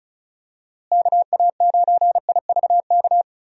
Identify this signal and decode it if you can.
Morse code, heard as a keyed tone, about 36 wpm, 700 Hz. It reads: KA9IVK
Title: USA Call Signs - 6 Characters - Repeated 2x - 35wpm